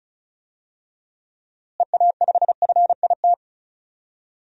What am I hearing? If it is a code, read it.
Morse code, heard as a keyed tone, about 36 wpm, 700 Hz. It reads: EA5FIT
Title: DXCC-100 Call Signs - 6 Characters - Repeated 3x - 35wpm